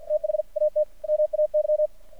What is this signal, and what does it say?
Morse code, heard as a keyed tone, about 52 wpm, spaced as for 45 wpm, 610 Hz. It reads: THAT MAY
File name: archived / CRSnet